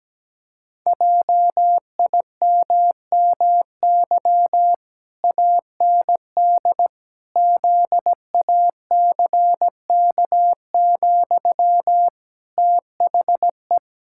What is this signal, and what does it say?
Morse code, heard as a keyed tone, about 17 wpm, 700 Hz. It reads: JIMMY AND ZACK, THE
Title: Pangram Challenge v2 - 17wpm